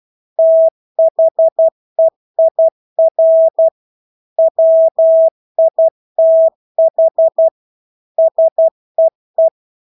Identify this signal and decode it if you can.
Morse code, heard as a keyed tone, about 12 wpm, 650 Hz. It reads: THEIR WITH SEE
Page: Engelske ord 12 wpm